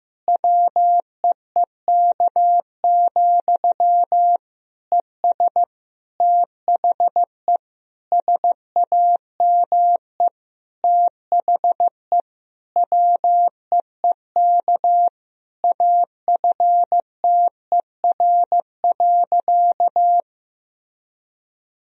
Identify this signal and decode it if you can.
Morse code, heard as a keyed tone, about 15 wpm, 700 Hz. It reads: WEEK, ES THE SAME THE WEEK AFTER.
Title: Sentences from Top 500 Words and Abbreviations 15wpm